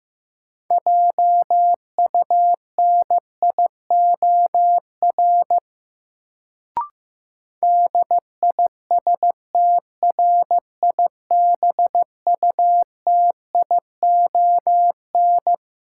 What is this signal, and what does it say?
Morse code, heard as a keyed tone, about 15 wpm, 700 Hz. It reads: JUNIOR DISTRIBUTION
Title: Sentences from Top 1000 Words - Review Part 7 - New Word Drill - 15wpm